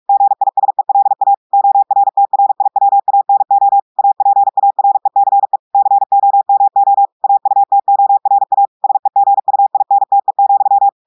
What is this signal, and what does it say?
Morse code, heard as a keyed tone, about 46 wpm, 800 Hz. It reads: GISELA ORTRIWANO APARECE COMO AUTORA SEGUINTE,
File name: portugais texte en clair 1 vitesse 45 mots minutes